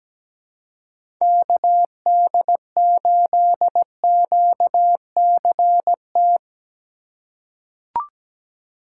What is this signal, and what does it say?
Morse code, heard as a keyed tone, about 17 wpm, 700 Hz. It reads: KD8QCT
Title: USA Call Signs - 6 Characters - 2x Character Spacing - 17wpm